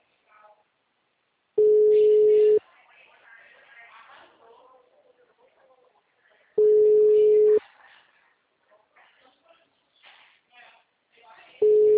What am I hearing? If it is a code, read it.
Ringback tone (Europe, 425 Hz), with speech.